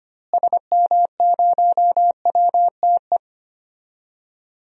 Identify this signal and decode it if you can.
Morse code, heard as a keyed tone, about 25 wpm, 700 Hz. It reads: SM0WTE
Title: DXCC-100 Call Signs - 6 Characters - Repeated 2x - 25wpm